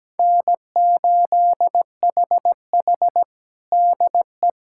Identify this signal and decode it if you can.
Morse code, heard as a keyed tone, about 17 wpm, 700 Hz. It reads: N8HH DE
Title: USA Callsign DE Callsign - 4 Characters - 17wpm